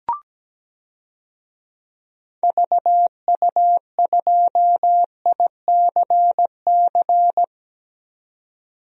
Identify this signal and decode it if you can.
Morse code, heard as a keyed tone, about 17 wpm, 700 Hz. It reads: VU2ICC
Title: DXCC-100 Call Signs - 6 Characters - 17wpm